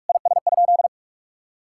Morse code, heard as a keyed tone, about 41 wpm, 700 Hz. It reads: IS?
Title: Sentences from Top 1000 Words - Review Part 6 - 45wpm